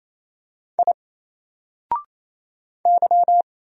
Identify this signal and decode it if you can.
Morse code, heard as a keyed tone, about 28 wpm, 700 Hz. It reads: I Y
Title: Single Character (Letter:Number:Punctuation:Prosign) 28wpm